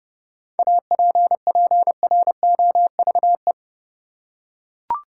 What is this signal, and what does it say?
Morse code, heard as a keyed tone, about 30 wpm, 700 Hz. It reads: APPROVE
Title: Seven Letter Words - 30wpm